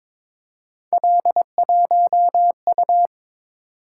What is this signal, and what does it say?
Morse code, heard as a keyed tone, about 22 wpm, 700 Hz. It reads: L1U
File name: practice / 22wpm / Three Letter-Number Combinations - 22wpm